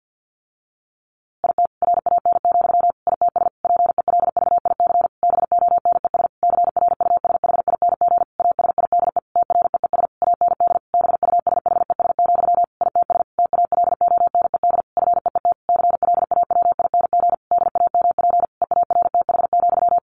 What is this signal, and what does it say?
Morse code, heard as a keyed tone, about 50 wpm, 700 Hz. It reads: IT RAN, ITS PELVIC BONES CRUSHING ASIDE TREES AND BUSHES, ITS TALONED FEET CLAWING DAMP EARTH,
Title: A Sound Of Thunder 50wpm